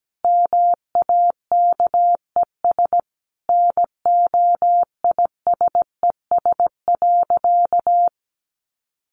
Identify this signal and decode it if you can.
Morse code, heard as a keyed tone, about 17 wpm, 700 Hz. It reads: MAKES NOISES.
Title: Easy Phrases 17wpm